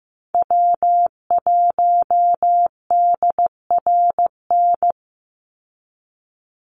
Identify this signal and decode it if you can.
Morse code, heard as a keyed tone, about 15 wpm, 700 Hz. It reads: W1DRN
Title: USA Call Signs 15wpm